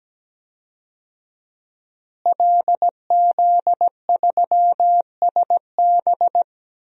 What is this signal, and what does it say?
Morse code, heard as a keyed tone, about 17 wpm, 700 Hz. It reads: LZ3SB
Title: DXCC-100 Call Signs - 5 Characters - Repeated 3x - 17wpm